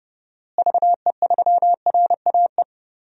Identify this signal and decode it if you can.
Morse code, heard as a keyed tone, about 30 wpm, 700 Hz. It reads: VE3RAE
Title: DXCC-100 Call Signs - 6 Characters - Repeated 2x - 30wpm